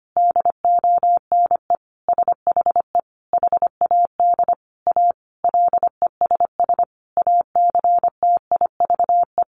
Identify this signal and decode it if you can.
Morse code, heard as a keyed tone, about 25 wpm, 700 Hz. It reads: DONE SHE HAD A LESS ACTIVE
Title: Sentences from Top 900 Words 25wpm